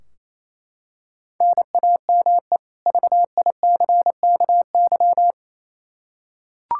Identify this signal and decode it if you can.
Morse code, heard as a keyed tone, about 28 wpm, 700 Hz. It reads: NAME VICKY
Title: Practice QSO - Exchange 1 - Beginning - 28wpm